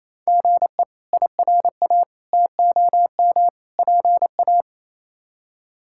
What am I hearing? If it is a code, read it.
Morse code, heard as a keyed tone, about 28 wpm, 700 Hz. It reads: GE IRA TOM PA
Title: K1USN SST QSO Practice - 28wpm